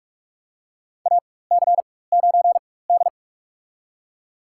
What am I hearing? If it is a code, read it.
Morse code, heard as a keyed tone, about 46 wpm, spaced as for 21 wpm, 700 Hz. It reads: AC9D
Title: USA Call Signs - 4 Characters - 4x Character Spacing - 45wpm